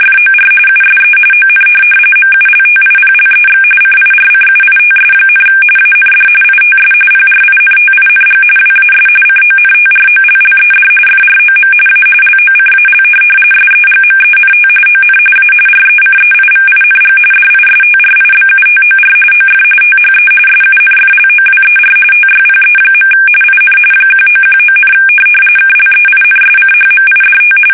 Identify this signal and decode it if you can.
RTTY, about 45.45 baud. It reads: FOR USE ON MULTIPLEX CIRCUITS, AND ITS PRINTING WAS CONTROLLED FROM A LOCAL SEGMENT ON A RECEIVING DISTRIBUTOR OF THE SUNFLOWER TYPE. IN 1919, KLEINSCHMIDT
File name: fsk850 / teletype-corp